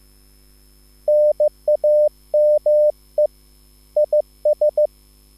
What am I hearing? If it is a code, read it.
Morse code, heard as a keyed tone, about 15 wpm, 600 Hz. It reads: NAME IS